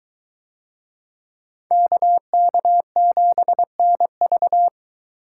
Morse code, heard as a keyed tone, about 22 wpm, 700 Hz. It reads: KK7NV